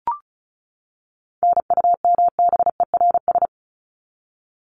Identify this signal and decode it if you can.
Morse code, heard as a keyed tone, about 34 wpm, 700 Hz. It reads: NUMBERS
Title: ES:S Suffix - Top 100 Words with Base Words 35wpm